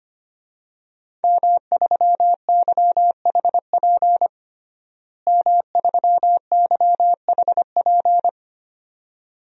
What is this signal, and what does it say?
Morse code, heard as a keyed tone, about 25 wpm, 700 Hz. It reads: M3YHP M3YHP
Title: DXCC-100 Call Signs - 5 Characters - Repeated 2x - 25wpm